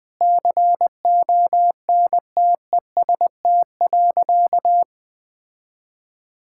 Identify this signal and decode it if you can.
Morse code, heard as a keyed tone, about 20 wpm, 700 Hz. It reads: CONTEST.
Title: Sentences from Top 1600 Words 20wpm